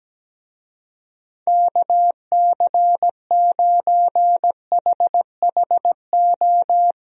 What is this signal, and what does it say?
Morse code, heard as a keyed tone, about 17 wpm, 700 Hz. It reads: KC9HHO
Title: USA Call Signs - 6 Characters - Repeated 2x - 17wpm